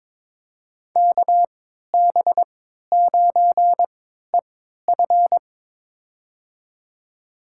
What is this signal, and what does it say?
Morse code, heard as a keyed tone, about 21 wpm, spaced as for 12 wpm, 700 Hz. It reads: KB9EF
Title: USA Call Signs - 3x Character Spacing - 22wpm